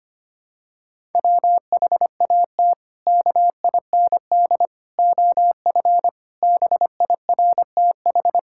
Morse code, heard as a keyed tone, about 25 wpm, 700 Hz. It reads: WHAT KIND OF BIRTH
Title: Sentences from Top 1000 Words - Review Part 5 - 25wpm